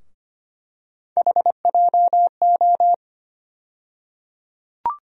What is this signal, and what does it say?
Morse code, heard as a keyed tone, about 25 wpm, 700 Hz. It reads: HJO
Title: USA Call Sign Postfixes - 3 Letters 25wpm